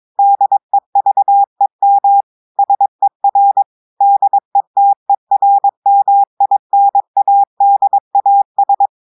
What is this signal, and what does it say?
Morse code, heard as a keyed tone, about 22 wpm, 800 Hz. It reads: DEVEM SER DETERMINADAS
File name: audios / portugais / portugais texte en clair 2 vitesse 22 mots minutes